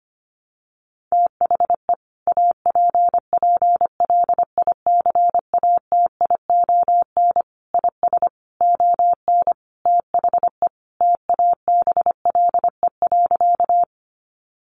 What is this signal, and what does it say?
Morse code, heard as a keyed tone, about 25 wpm, 700 Hz. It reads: THE APPLICATION IS ON THE TABLE.
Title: Sentences from Top 1200 Words 25wpm